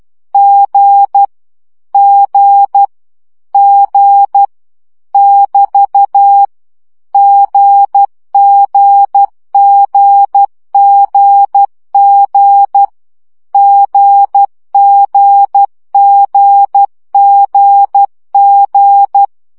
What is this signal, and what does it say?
Morse code, heard as a keyed tone, about 12 wpm, 800 Hz. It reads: G G G <BT> GGGGG GGGGG
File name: CW gyakorlas g 10 baud